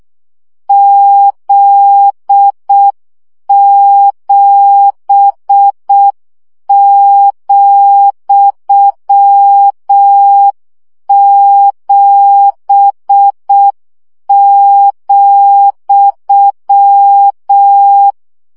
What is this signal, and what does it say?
Morse code, heard as a keyed tone, about 6 wpm, 800 Hz. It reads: Z7,7,
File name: CW gyakorlas z 7 vesszo 5 baud